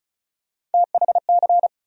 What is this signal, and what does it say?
Morse code, heard as a keyed tone, about 34 wpm, 700 Hz. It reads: THC
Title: USA Call Sign Postfixes - 3 Letters 35wpm